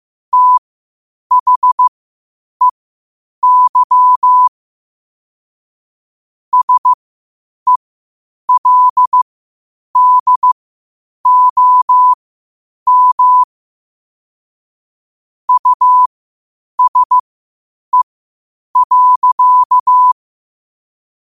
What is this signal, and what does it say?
Morse code, heard as a keyed tone, about 15 wpm, spaced as for 8 wpm, 1000 Hz. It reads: THEY SELDOM USE.
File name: qotdfiles / 8wpm